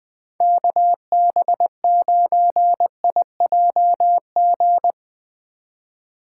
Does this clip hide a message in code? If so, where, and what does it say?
Yes: Morse code from 0.4 to 4.9 s, heard as a keyed tone, about 21 wpm, 700 Hz. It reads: KB9IJG